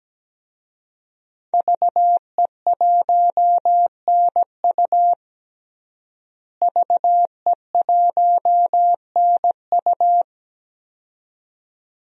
Morse code, heard as a keyed tone, about 17 wpm, 700 Hz. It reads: VE1NU VE1NU
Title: DXCC-100 Call Signs - 5 Characters - Repeated 2x - 17wpm